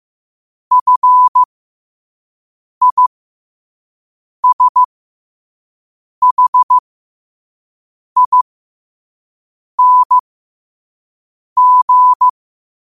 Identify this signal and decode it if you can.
Morse code, heard as a keyed tone, about 15 wpm, spaced as for 5 wpm, 1000 Hz. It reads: FISHING
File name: qotdfiles / 5wpm